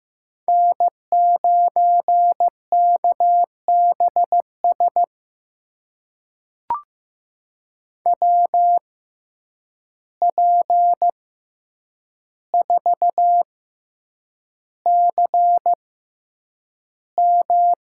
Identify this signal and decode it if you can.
Morse code, heard as a keyed tone, about 15 wpm, spaced as for 8 wpm, 700 Hz. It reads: N9KBS WP4CM
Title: USA Call Signs - 5 Characters - 6x Character Spacing - 15wpm